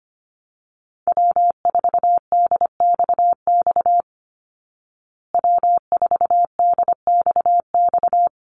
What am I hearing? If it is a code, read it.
Morse code, heard as a keyed tone, about 25 wpm, 700 Hz. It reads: W4DXX W4DXX
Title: Call Signs Repeated 2x - 25wpm